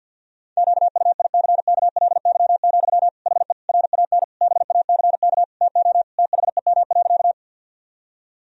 Morse code, heard as a keyed tone, about 50 wpm, 700 Hz. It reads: QUICKLY, HE RAN BACK TO THEM.
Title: Sentences from Top 1000 Words - Review Part 3 - 50wpm